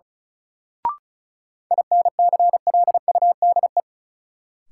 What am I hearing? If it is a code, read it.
Morse code, heard as a keyed tone, about 36 wpm, 700 Hz. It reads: INCLUDE
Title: Sentences from Top 1000 Words - Review Part 3 - New Word Drill - 35wpm